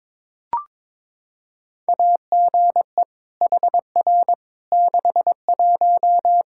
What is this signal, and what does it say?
Morse code, heard as a keyed tone, about 22 wpm, 700 Hz. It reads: AGE HR 61
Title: Practice QSO - Exchange 3 - More Info - 22wpm